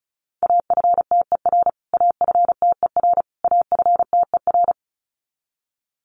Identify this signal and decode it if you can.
Morse code, heard as a keyed tone, about 34 wpm, 700 Hz. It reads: AFTER AFTER AFTER
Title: Top 300 Words Repeated 35wpm